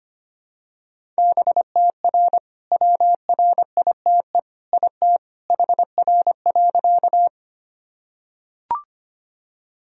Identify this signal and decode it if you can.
Morse code, heard as a keyed tone, about 25 wpm, 700 Hz. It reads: BTR WRITE IT HR.
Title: Sentences from Top 400 Words and Abbreviations 25wpm